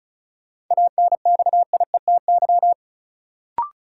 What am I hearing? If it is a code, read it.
Morse code, heard as a keyed tone, about 36 wpm, 700 Hz. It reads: ANXIETY
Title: Sentences from Top 1600 Words - New Word Drill - 35wpm